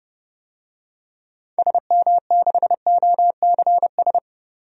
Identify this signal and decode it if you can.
Morse code, heard as a keyed tone, about 30 wpm, 700 Hz. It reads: SM6OCS
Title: DXCC-100 Call Signs - 6 Characters - 30wpm